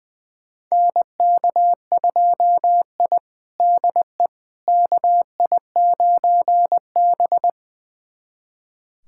Morse code, heard as a keyed tone, about 20 wpm, 700 Hz. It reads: NK2I DE KI9B
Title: USA Callsign DE Callsign - 4 Characters - 20wpm